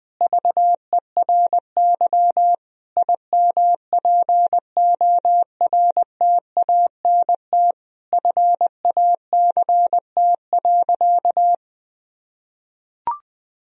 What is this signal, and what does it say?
Morse code, heard as a keyed tone, about 20 wpm, 700 Hz. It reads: VERY IMPORTANT FACT.
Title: Sentences from Top 1000 Words - Review Part 3 - 20wpm